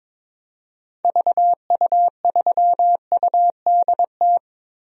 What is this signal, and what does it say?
Morse code, heard as a keyed tone, about 22 wpm, 700 Hz. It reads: VU3UDT
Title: DXCC-100 Call Signs - 6 Characters - 22wpm